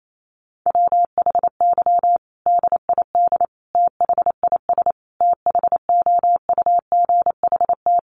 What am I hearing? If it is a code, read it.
Morse code, heard as a keyed tone, about 28 wpm, 700 Hz. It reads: WHY DID THIS THOUGHT
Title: Sentences from Top 900 Words 28wpm